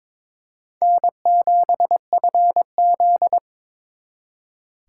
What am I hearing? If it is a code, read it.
Morse code, heard as a keyed tone, about 26 wpm, 700 Hz. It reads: N7FZ
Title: Speed Racing - USA 1x2 Call Signs 22wpm